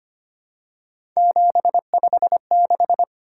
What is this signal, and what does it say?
Morse code, heard as a keyed tone, about 25 wpm, 700 Hz. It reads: 756
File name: Three Letter-Number Combinations - 25wpm